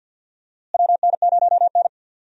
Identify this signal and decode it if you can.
Morse code, heard as a keyed tone, about 47 wpm, 700 Hz. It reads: WN0N
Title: USA Call Signs - 4 Characters - Repeated 2x - 50wpm